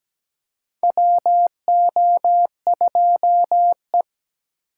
Morse code, heard as a keyed tone, about 17 wpm, 700 Hz. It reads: WO2E
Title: K1USN SST QSO Practice - 17wpm